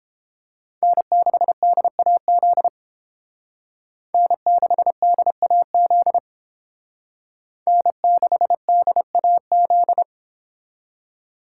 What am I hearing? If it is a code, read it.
Morse code, heard as a keyed tone, about 28 wpm, 700 Hz. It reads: N6DAZ N6DAZ N6DAZ